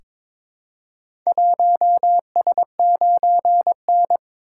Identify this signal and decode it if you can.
Morse code, heard as a keyed tone, about 22 wpm, 700 Hz. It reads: 1S9N
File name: practice / 22wpm / Four Letter-Number Combinations - 22wpm